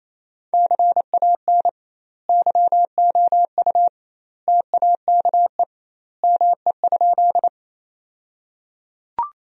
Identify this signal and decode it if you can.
Morse code, heard as a keyed tone, about 28 wpm, 700 Hz. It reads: CAN YOU TAKE ME?
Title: 2x Word-Spacing - Sentences from Top 200 Words - 28wpm